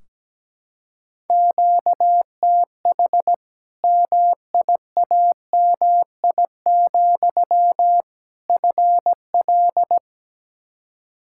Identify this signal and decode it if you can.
Morse code, heard as a keyed tone, about 17 wpm, 700 Hz. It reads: QTH MIAMI, FL
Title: Practice QSO - Exchange 1 - Beginning - 17wpm